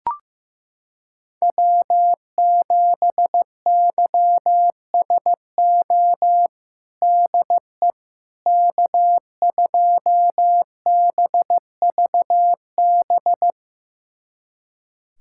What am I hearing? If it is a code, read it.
Morse code, heard as a keyed tone, about 16 wpm, 700 Hz. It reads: W7YSO DE K2BVB
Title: USA Callsign DE Callsign - 5 Characters - 15wpm